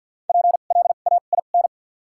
Morse code, heard as a keyed tone, about 53 wpm, spaced as for 37 wpm, 700 Hz. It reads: PLAIN
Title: 2x Character Spacing - Top 500 Words - 50wpm